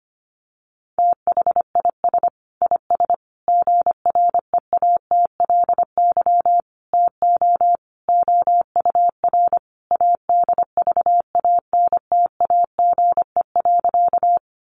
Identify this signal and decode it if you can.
Morse code, heard as a keyed tone, about 25 wpm, 700 Hz. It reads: THIS IS GREATLY TO OUR ADVANTAGE.
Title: Sentences from Top 1400 Words 25wpm